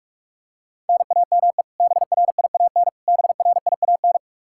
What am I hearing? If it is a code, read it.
Morse code, heard as a keyed tone, about 41 wpm, 700 Hz. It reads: NAME BRIAN BRIAN
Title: K1USN SST QSO Practice 45wpm